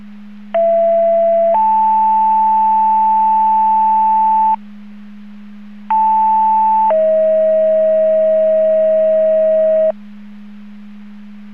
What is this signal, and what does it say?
Two-tone page, heard as a one-second tone, then a three-second tone, about 670 then 880 Hz, about 880 then 640 Hz. It is heard from 0.5 to 9.9 s.